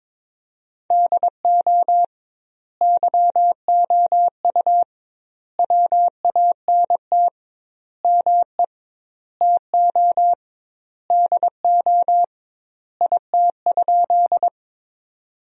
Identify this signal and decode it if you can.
Morse code, heard as a keyed tone, about 22 wpm, 700 Hz. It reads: DO YOU WANT ME TO DO IT?
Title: 2x Word-Spacing - Sentences from Top 200 Words - 22wpm